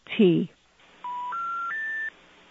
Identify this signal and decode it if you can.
Special information tone, with speech.